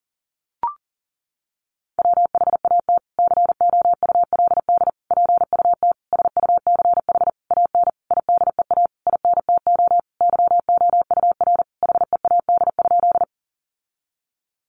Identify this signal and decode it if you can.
Morse code, heard as a keyed tone, about 40 wpm, 700 Hz. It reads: WHAT COULD PUT SUCH AN IDEA INTO YOUR HEAD?
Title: Sentences from Top 600 Words 40wpm